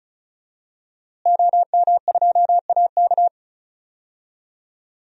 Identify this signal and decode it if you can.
Morse code, heard as a keyed tone, about 36 wpm, 700 Hz. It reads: OM2AK
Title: DXCC-100 Call Signs - 5 Characters - Repeated 2x - 35wpm